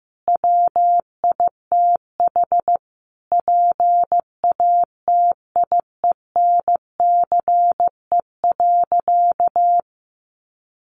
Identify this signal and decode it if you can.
Morse code, heard as a keyed tone, about 15 wpm, 700 Hz. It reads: WITH PATIENCE.
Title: Sentences from Top 1300 Words 15wpm